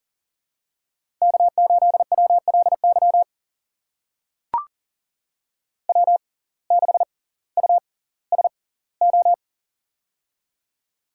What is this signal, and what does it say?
Morse code, heard as a keyed tone, about 40 wpm, spaced as for 20 wpm, 700 Hz. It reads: K8WLY W6USO
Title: USA Call Signs - 5 Characters - 6x Character Spacing - 40wpm